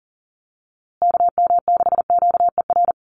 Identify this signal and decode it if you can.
Morse code, heard as a keyed tone, about 40 wpm, 700 Hz. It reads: KM6QER
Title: USA Call Signs Repeated 3x 40wpm